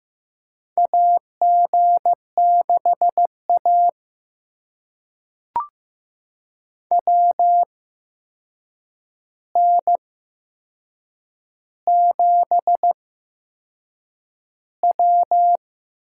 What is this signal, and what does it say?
Morse code, heard as a keyed tone, about 15 wpm, spaced as for 6 wpm, 700 Hz. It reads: AG6A WN7W
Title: USA Call Signs - 4 Characters - 8x Character Spacing - 15wpm